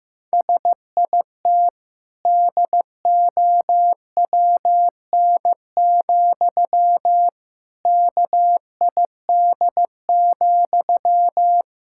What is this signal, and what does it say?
Morse code, heard as a keyed tone, about 15 wpm, 700 Hz. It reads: SIT DOWN, KID,
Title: Sentences from Top 1600 Words 15wpm